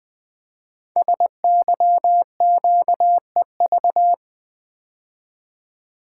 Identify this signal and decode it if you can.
Morse code, heard as a keyed tone, about 20 wpm, 700 Hz. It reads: SYQEV
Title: Code Groups - 5 Letters - 20wpm